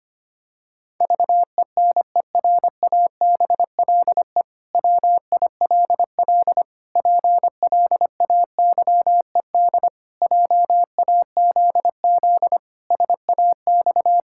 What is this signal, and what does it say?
Morse code, heard as a keyed tone, about 25 wpm, 700 Hz. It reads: VENERABLE WILL PLAYED JAZZ SAX
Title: Pangram Challenge v2 - 25wpm